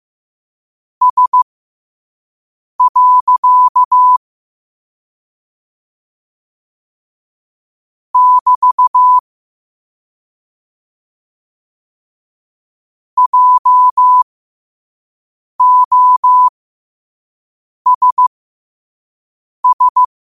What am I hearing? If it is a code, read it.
Morse code, heard as a keyed tone, about 15 wpm, spaced as for 5 wpm, 1000 Hz. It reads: S. <BT> JOSS
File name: qotdfiles / 5wpm